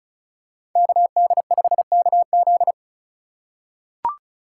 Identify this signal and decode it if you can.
Morse code, heard as a keyed tone, about 36 wpm, 700 Hz. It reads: KD5KZ
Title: USA Call Signs - 5 Characters - 2x Character Spacing - 35wpm